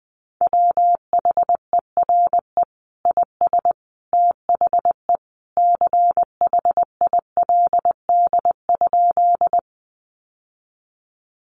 Morse code, heard as a keyed tone, about 20 wpm, 700 Hz. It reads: WHERE IS THE CHILD?
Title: Sentences from Top 800 Words 20wpm